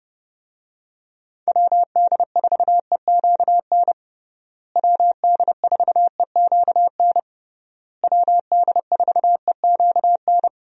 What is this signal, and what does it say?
Morse code, heard as a keyed tone, about 32 wpm, 700 Hz. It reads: WD4EQN WD4EQN WD4EQN
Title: USA Call Signs - 6 Characters - Repeated 3x - 30wpm